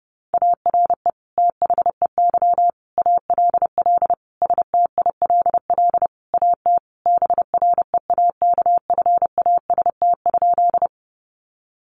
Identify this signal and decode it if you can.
Morse code, heard as a keyed tone, about 30 wpm, 700 Hz. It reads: ARE THEY ALL STILL AT BREAKFAST?
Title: Sentences from Top 1500 Words 30wpm